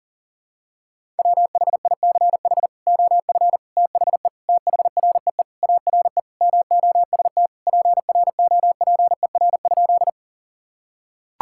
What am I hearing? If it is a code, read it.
Morse code, heard as a keyed tone, about 40 wpm, 700 Hz. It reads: WHICH OF THE THREE ARE MOST PROPER?
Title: Sentences from Top 1000 Words - Review Part 5 - 40wpm